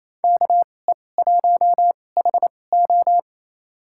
Morse code, heard as a keyed tone, about 28 wpm, spaced as for 20 wpm, 700 Hz. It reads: KE1HO
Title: USA Call Signs - 5 Characters - 2x Character Spacing - 28wpm